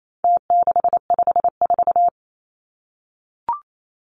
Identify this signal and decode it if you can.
Morse code, heard as a keyed tone, about 28 wpm, 700 Hz. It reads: T654
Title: Numbers - Contest Style - 4-Digit 28wpm